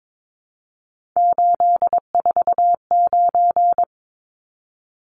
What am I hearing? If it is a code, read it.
Morse code, heard as a keyed tone, about 22 wpm, 700 Hz. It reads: 849
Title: Numbers - 3-Digit 22wpm